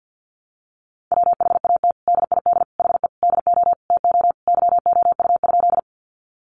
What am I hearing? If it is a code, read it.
Morse code, heard as a keyed tone, about 50 wpm, 700 Hz. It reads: WHAT DID HE DO TO YOU?
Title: Sentences from Top 100 Words 50wpm